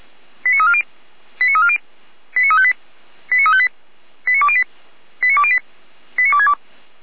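Five-tone selective call (ZVEI), heard as quick runs of five tones, tones of about 70 ms. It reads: 89340 89340 89358 89358 89208 89208 89262
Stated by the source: ZVEI five-tone selective call received by SDR, logged on the server of feuerwehr-orlamuende (fire brigade Orlamünde).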